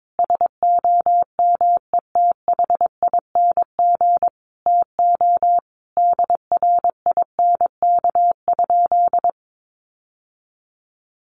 Morse code, heard as a keyed tone, about 22 wpm, 700 Hz. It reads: SOMETHING TO DRINK?
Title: Sentences from Top 1500 Words 22wpm